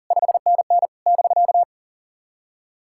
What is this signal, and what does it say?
Morse code, heard as a keyed tone, about 40 wpm, 660 Hz. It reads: 5NN <BK>
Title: SOTA QSO Practice 40wpm